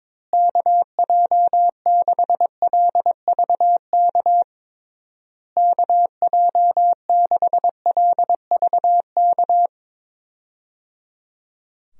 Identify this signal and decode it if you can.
Morse code, heard as a keyed tone, about 22 wpm, 700 Hz. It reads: KJ6LVK KJ6LVK
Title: USA Call Signs - 6 Characters - Repeated 3x - 22wpm